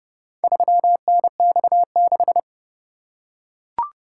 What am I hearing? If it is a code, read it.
Morse code, heard as a keyed tone, about 29 wpm, 700 Hz. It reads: 3NX6